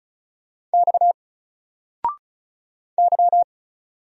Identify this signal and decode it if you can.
Morse code, heard as a keyed tone, about 35 wpm, 700 Hz. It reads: X Y
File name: Single-Letter-Number-35wpm